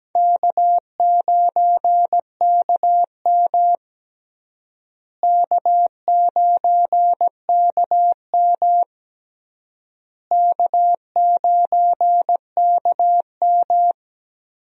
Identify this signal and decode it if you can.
Morse code, heard as a keyed tone, about 17 wpm, 700 Hz. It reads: K9KM K9KM K9KM
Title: USA Call Signs - 4 Characters - Repeated 3x - 17wpm